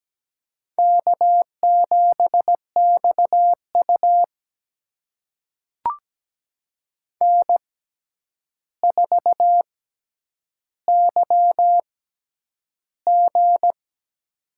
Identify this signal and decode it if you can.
Morse code, heard as a keyed tone, about 17 wpm, spaced as for 9 wpm, 700 Hz. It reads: K7XU N4YG
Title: USA Call Signs - 4 Characters - 6x Character Spacing - 17wpm